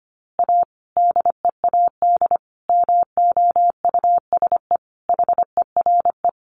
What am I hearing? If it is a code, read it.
Morse code, heard as a keyed tone, about 25 wpm, 700 Hz. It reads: A DEAD MOUSE HERE
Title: A Sound Of Thunder 25wpm